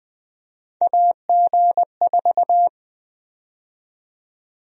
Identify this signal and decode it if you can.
Morse code, heard as a keyed tone, about 21 wpm, 700 Hz. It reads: AG4
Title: USA Call Sign Prefixes - 20wpm